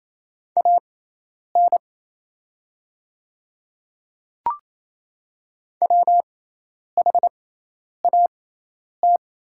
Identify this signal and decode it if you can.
Morse code, heard as a keyed tone, about 28 wpm, spaced as for 10 wpm, 700 Hz. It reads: AN WHAT
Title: 6x Character Spacing - Top 100 Words - 28wpm